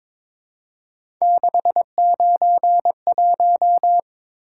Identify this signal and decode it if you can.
Morse code, heard as a keyed tone, about 23 wpm, 700 Hz. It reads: 691
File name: practice / 22wpm / Three Letter-Number Combinations - 22wpm